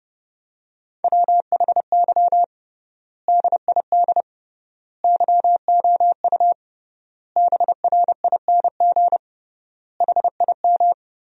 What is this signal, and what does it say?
Morse code, heard as a keyed tone, about 32 wpm, 700 Hz. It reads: WHY DID YOU BRING HIM
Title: 3x Word-Spacing - Sentences from Top 500 Words - 30wpm